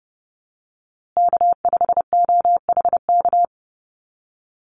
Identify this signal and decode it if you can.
Morse code, heard as a keyed tone, about 30 wpm, 700 Hz. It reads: K5OHK
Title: USA Call Signs Repeated 2x 30wpm